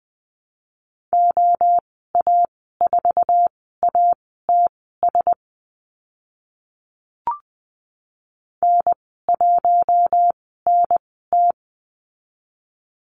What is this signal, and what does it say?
Morse code, heard as a keyed tone, about 20 wpm, spaced as for 14 wpm, 700 Hz. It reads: OA4ATS N1NT
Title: 069 Lesson 12g - Letter d - Call Signs 20wpm